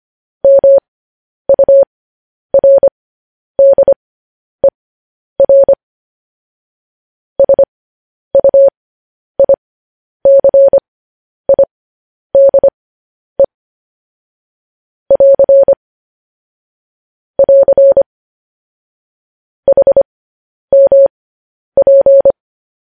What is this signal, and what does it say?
Morse code, heard as a keyed tone, about 25 wpm, spaced as for 10 wpm, 550 Hz. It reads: MURDER SUICIDE <AR> <AR> HMP
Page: news headlines in morse code at 10 wpm